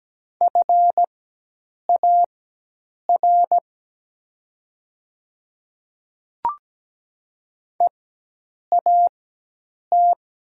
Morse code, heard as a keyed tone, about 17 wpm, spaced as for 8 wpm, 700 Hz. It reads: FAR EAT
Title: 4x Character Spacing - Top 300 Words - 17wpm